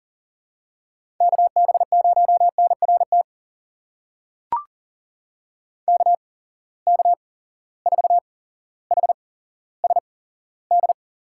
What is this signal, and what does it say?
Morse code, heard as a keyed tone, about 38 wpm, spaced as for 17 wpm, 700 Hz. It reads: KB0NRT KK4HSD
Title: USA Call Signs - 6 Characters - 8x Character Spacing - 40wpm